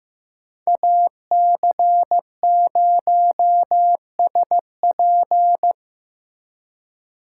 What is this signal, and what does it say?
Morse code, heard as a keyed tone, about 15 wpm, 700 Hz. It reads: AC0SP
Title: USA Call Signs - 4x Character Spacing - 15wpm